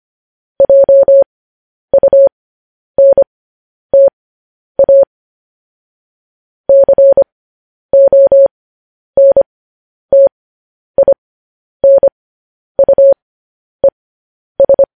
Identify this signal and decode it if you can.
Morse code, heard as a keyed tone, about 25 wpm, spaced as for 10 wpm, 550 Hz. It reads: JUNTA CONTINUES
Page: news-headlines-in-morse-code-at-10-wpm